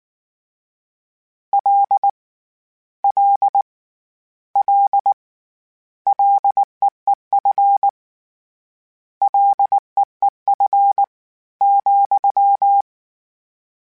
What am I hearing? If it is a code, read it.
Morse code, heard as a keyed tone, about 19 wpm, 800 Hz. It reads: L L L LEEF LEEF ,